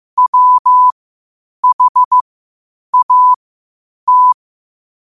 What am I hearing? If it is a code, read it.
Morse code, heard as a keyed tone, about 15 wpm, spaced as for 8 wpm, 1000 Hz. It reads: WHAT
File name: qotdfiles / 8wpm